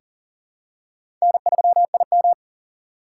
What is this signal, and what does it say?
Morse code, heard as a keyed tone, about 40 wpm, 690 Hz. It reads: N3IM
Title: USA Call Signs - 3x Character Spacing - 40wpm